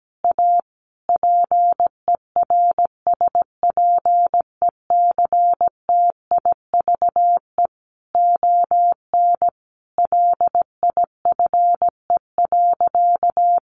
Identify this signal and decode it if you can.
Morse code, heard as a keyed tone, about 17 wpm, 700 Hz. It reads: A PERSPECTIVE ON LIFE.
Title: Sentences from Top 1200 Words 17wpm